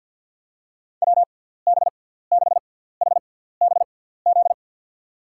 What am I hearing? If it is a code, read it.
Morse code, heard as a keyed tone, about 48 wpm, spaced as for 17 wpm, 700 Hz. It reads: WB6HBZ
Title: USA Call Signs - 6 Characters - 6x Character Spacing - 50wpm